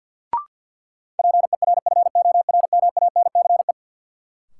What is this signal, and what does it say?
Morse code, heard as a keyed tone, about 52 wpm, 700 Hz. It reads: PERFORMANCE
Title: Sentences from Top 1000 Words - New Word Drill - 50wpm